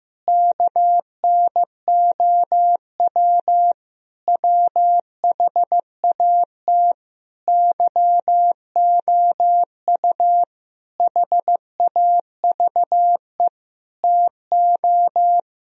Morse code, heard as a keyed tone, about 15 wpm, 700 Hz. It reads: KNOW WHAT YOU HAVE TO
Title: Sentences from Top 1000 Words - Review Part 3 - 15wpm